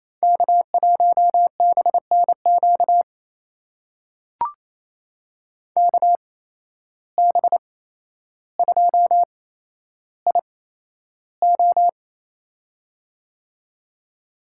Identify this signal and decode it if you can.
Morse code, heard as a keyed tone, about 28 wpm, spaced as for 12 wpm, 700 Hz. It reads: K1BNQ KB2IO
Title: USA Call Signs - 5 Characters - 8x Character Spacing - 28wpm